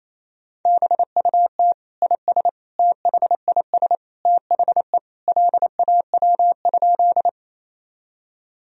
Morse code, heard as a keyed tone, about 28 wpm, 700 Hz. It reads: BUT IS THIS THE LAW?
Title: Sentences from Top 1000 Words - Review Part 3 - 28wpm